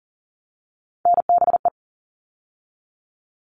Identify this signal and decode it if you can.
Morse code, heard as a keyed tone, about 38 wpm, 700 Hz. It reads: NBE
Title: All Cut Numbers - 3-Digit 40wpm